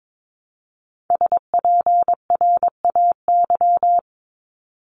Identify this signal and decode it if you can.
Morse code, heard as a keyed tone, about 22 wpm, 700 Hz. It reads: SPRAY
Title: Top 1751-2000 Words 22wpm